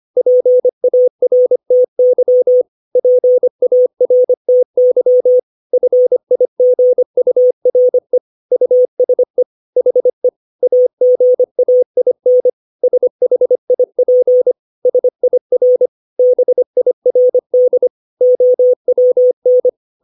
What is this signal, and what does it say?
Morse code, heard as a keyed tone, about 25 wpm, 500 Hz. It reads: PARTY PARTY FIGURE USE HE AGAIN SHIP SIR BIRD OWN
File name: 632 words / 632 Words 25WPM